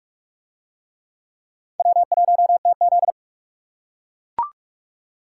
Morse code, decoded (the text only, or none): W1TZ